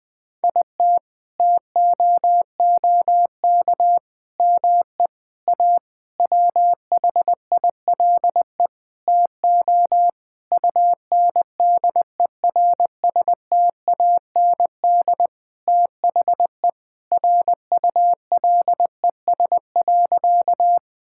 Morse code, heard as a keyed tone, about 20 wpm, 700 Hz. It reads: IT TOOK ME A WHILE TO UNDERSTAND THE RULES.
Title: Sentences with 6-Word Phrases - 20wpm